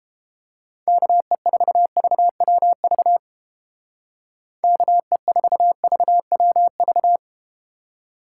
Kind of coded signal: Morse code

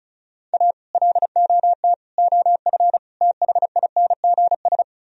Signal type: Morse code

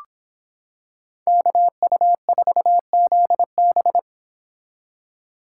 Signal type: Morse code

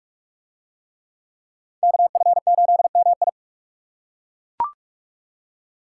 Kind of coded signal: Morse code

